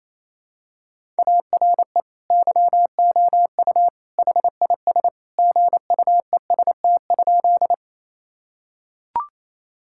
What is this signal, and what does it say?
Morse code, heard as a keyed tone, about 28 wpm, 700 Hz. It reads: ARE YOU HIS GUEST?
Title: Sentences from Top 1600 Words 28wpm